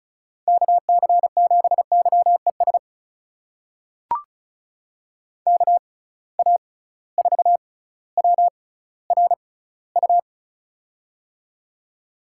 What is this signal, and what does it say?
Morse code, heard as a keyed tone, about 36 wpm, spaced as for 18 wpm, 700 Hz. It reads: KC7YES KA4WRU
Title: USA Call Signs - 6 Characters - 6x Character Spacing - 35wpm